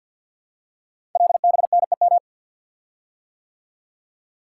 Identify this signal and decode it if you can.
Morse code, heard as a keyed tone, about 48 wpm, 700 Hz. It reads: LBNEM
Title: Code Groups - 5 Letters - 50wpm